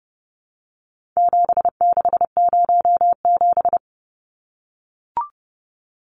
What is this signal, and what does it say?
Morse code, heard as a keyed tone, about 30 wpm, 700 Hz. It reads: 7607
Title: Numbers - 4-Digit 30wpm